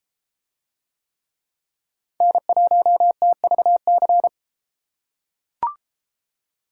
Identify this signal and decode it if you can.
Morse code, heard as a keyed tone, about 32 wpm, 700 Hz. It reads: N1TVC